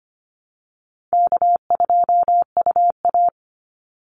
Morse code, heard as a keyed tone, about 25 wpm, 700 Hz. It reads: K2UA
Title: Easy Call Signs Repeated 3x - 25wpm